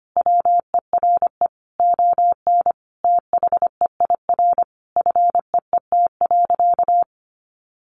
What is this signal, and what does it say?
Morse code, heard as a keyed tone, about 25 wpm, 700 Hz. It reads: WERE ON THEIR FEET.
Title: Sentences from Top 1000 Words 25wpm